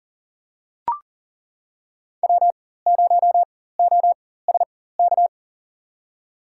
Morse code, heard as a keyed tone, about 40 wpm, spaced as for 19 wpm, 700 Hz. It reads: W0OSK
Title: USA Call Signs - 5 Characters - 4x Character Spacing - 40wpm